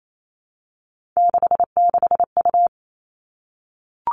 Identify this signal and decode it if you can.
Morse code, heard as a keyed tone, about 28 wpm, 700 Hz. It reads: 66U
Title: Three Challenging Letter-Number Combinations 28wpm